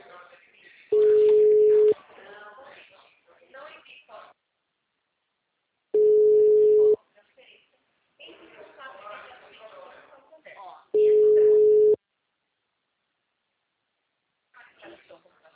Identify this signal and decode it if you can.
Ringback tone (Europe, 425 Hz), with speech.
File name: out- -2002- - - .18893